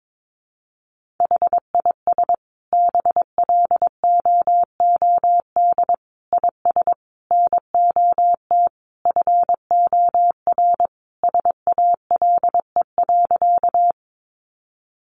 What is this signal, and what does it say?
Morse code, heard as a keyed tone, about 22 wpm, 700 Hz. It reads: HIS BLOOD IS NOT FOR SALE.
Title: Sentences from Top 1400 Words 22wpm